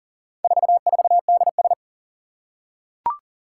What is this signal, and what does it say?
Morse code, heard as a keyed tone, about 40 wpm, 700 Hz. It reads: 44DS
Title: Four Challenging Letter-Number Combinations 40wpm